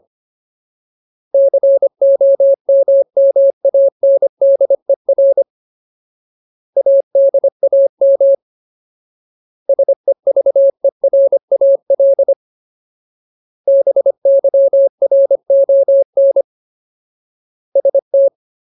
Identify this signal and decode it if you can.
Morse code, heard as a keyed tone, about 25 wpm, 550 Hz. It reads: COMMANDER ADAM SEVERAL BYRON ST
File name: Ultimate Word List / 25WPM Ultimate